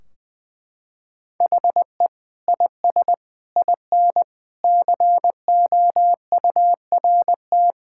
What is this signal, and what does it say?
Morse code, heard as a keyed tone, about 20 wpm, 700 Hz. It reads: HE IS IN COURT
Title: Sentences from Top 1600 Words 20wpm